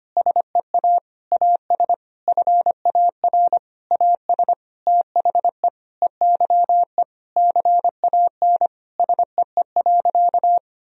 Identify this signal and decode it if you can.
Morse code, heard as a keyed tone, about 25 wpm, 700 Hz. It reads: SEA AS FAR AS THE EYE CAN SEE.
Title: Sentences from Top 1000 Words - Review Part 1 - 25wpm